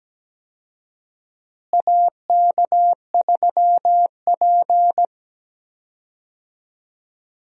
Morse code, heard as a keyed tone, about 17 wpm, 700 Hz. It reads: AK3P
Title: USA Call Signs - 4 Characters - Repeated 2x - 17wpm